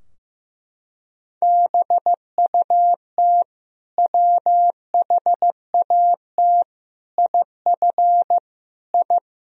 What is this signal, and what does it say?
Morse code, heard as a keyed tone, about 16 wpm, 700 Hz. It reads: BUT WHAT IF I
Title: Sentences from Top 1000 Words - Review Part 2 - 15wpm